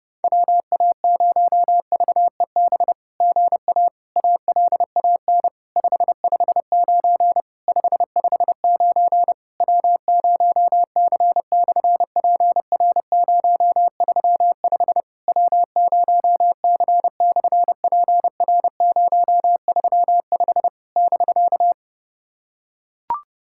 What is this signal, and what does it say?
Morse code, heard as a keyed tone, about 30 wpm, 700 Hz. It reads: WA0VEB GA ALAN 559 559 W0C/PR035 W0C/PR035 <BK>
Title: SOTA QSO Practice 30wpm